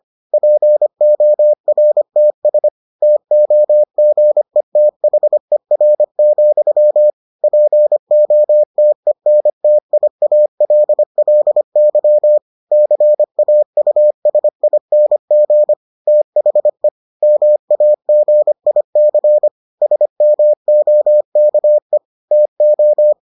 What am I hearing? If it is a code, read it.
Morse code, heard as a keyed tone, about 25 wpm, 600 Hz. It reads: PORTS TOGETHER, POTENTIALLY CAUSING THE MAGIC SMOKE TO